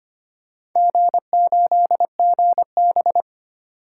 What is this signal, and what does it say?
Morse code, heard as a keyed tone, about 25 wpm, 700 Hz. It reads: G8GB
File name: Four Letter-Number Combinations - 25wpm